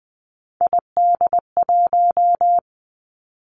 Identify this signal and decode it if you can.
Morse code, heard as a keyed tone, about 20 wpm, 700 Hz. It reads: ID1
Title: Three Challenging Letter-Number Combinations 20wpm